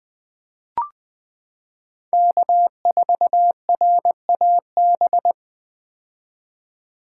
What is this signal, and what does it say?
Morse code, heard as a keyed tone, about 20 wpm, 700 Hz. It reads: K4RAB
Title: K1USN SST QSO Practice - 20wpm